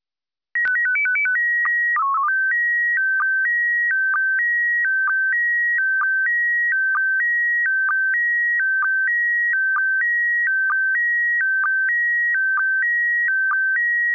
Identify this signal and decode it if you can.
SSTV (slow-scan television), PD 290.
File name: lu4aao / PD290